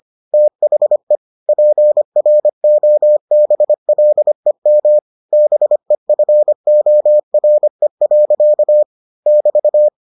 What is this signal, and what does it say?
Morse code, heard as a keyed tone, about 25 wpm, 600 Hz. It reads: THE PROBLEM BEFORE. <BT>